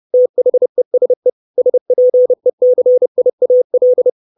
Morse code, heard as a keyed tone, about 30 wpm, 500 Hz. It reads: THESE SPECIAL
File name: 632 Words 30WPM